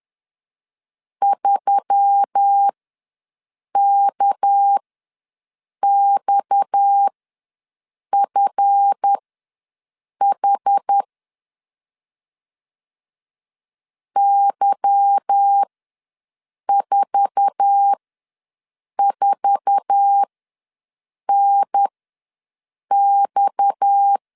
Morse code, heard as a keyed tone, about 10 wpm, spaced as for 6 wpm, 800 Hz. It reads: 3KXFH Y44NX
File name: LEZIONI / 8 YJ45 / 8 YJ4 ws 10wpm 4020